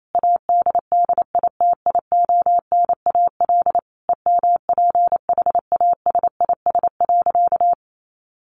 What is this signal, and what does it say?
Morse code, heard as a keyed tone, about 28 wpm, 700 Hz. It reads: ADDITIONAL EMPHASIS.
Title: Sentences from Top 1400 Words 28wpm